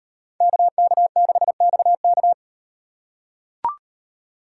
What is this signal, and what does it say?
Morse code, heard as a keyed tone, about 38 wpm, 700 Hz. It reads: KK6XK